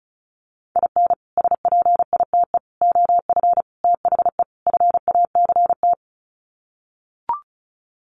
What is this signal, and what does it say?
Morse code, heard as a keyed tone, about 34 wpm, 700 Hz. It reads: IN SPITE OF THE FACT
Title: Sets of 5 Words Encore2 35wpm